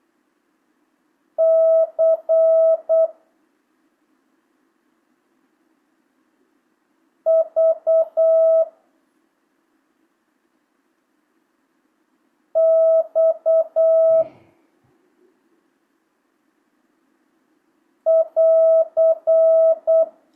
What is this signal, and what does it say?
Morse code, heard as a keyed tone, about 8 wpm, 650 Hz. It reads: C V X <AR>
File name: jh8cbh / 2025cw